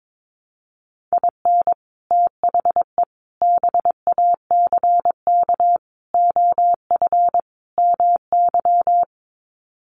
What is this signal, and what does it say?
Morse code, heard as a keyed tone, about 22 wpm, 700 Hz. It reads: IN THE BACK OF MY
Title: Sets of 5 Words Encore 22wpm